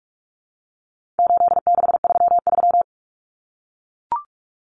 Morse code, heard as a keyed tone, about 44 wpm, 700 Hz. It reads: 8633
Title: Numbers - 4-Digit 45wpm